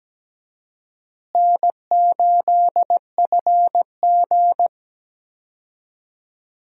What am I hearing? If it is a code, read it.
Morse code, heard as a keyed tone, about 17 wpm, 700 Hz. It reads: N8FG
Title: USA Call Signs - 4 Characters - Repeated 2x - 17wpm